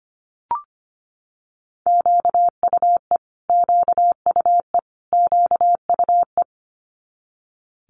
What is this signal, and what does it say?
Morse code, heard as a keyed tone, about 25 wpm, 700 Hz. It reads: QUE QUE QUE
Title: Multi-letter Phonemes Repeated 3x 25wpm